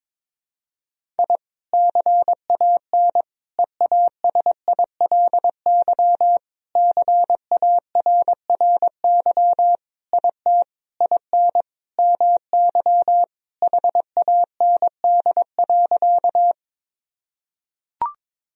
Morse code, heard as a keyed tone, about 22 wpm, 700 Hz. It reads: I CAN EASILY CARRY IT IN MY HAND.
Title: Sentences from Top 1000 Words - Review Part 1 - 22wpm